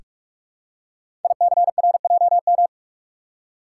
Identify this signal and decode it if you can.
Morse code, heard as a keyed tone, about 44 wpm, 690 Hz. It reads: ICRJM
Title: Code Groups - 5 Letters - 45wpm